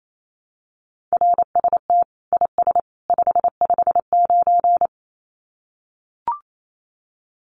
Morse code, heard as a keyed tone, about 28 wpm, 700 Hz. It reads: RST IS 559
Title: Common QSO Phrases 28wpm